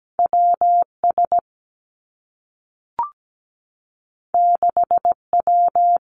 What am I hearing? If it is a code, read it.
Morse code, heard as a keyed tone, about 17 wpm, 700 Hz. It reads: WS 6W
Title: Two Challenging Letter-Number Combinations 17wpm